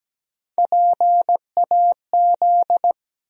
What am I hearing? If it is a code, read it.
Morse code, heard as a keyed tone, about 17 wpm, 700 Hz. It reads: PAZ
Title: USA Call Sign Postfixes - 3 Letters 17wpm